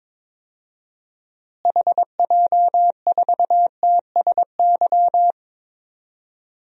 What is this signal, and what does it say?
Morse code, heard as a keyed tone, about 22 wpm, 700 Hz. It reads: HJ4TSY
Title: DXCC-100 Call Signs - 6 Characters - Repeated 3x - 22wpm